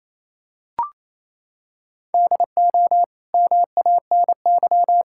Morse code, heard as a keyed tone, about 28 wpm, 700 Hz. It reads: DO MANY
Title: Sentences from Top 1000 Words - Review Part 6 - 28wpm